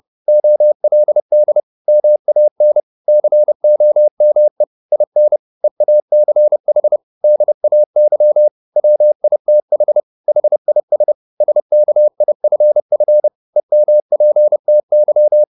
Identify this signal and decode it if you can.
Morse code, heard as a keyed tone, about 30 wpm, 600 Hz. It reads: OLD MAN COME IN EACH DAY WITH HIS SKIFF EMPTY
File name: the old man and the sea / the old man and the sea 1 30wpm